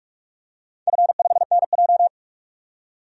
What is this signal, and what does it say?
Morse code, heard as a keyed tone, about 46 wpm, 700 Hz. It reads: F5NJ
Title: DXCC-100 Call Signs - 4 Characters - 45wpm